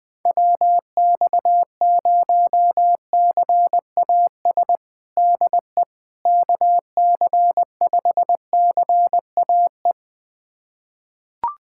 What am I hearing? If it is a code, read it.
Morse code, heard as a keyed tone, about 21 wpm, 700 Hz. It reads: WX0CAS DE KC5CAE
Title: USA Callsign DE Callsign - 6 Characters - 20wpm